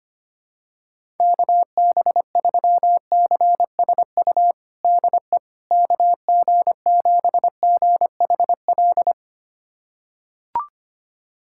Morse code, heard as a keyed tone, about 25 wpm, 700 Hz. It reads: KB3CSU DE KG7GHL
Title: USA Callsign DE Callsign - 6 Characters - 25wpm